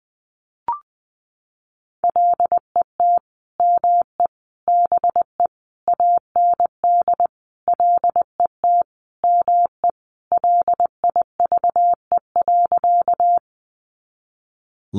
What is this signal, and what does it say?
Morse code, heard as a keyed tone, about 20 wpm, 700 Hz. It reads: LET ME BE AND LET ME LIVE.
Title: Sentences from Top 300 Words 20wpm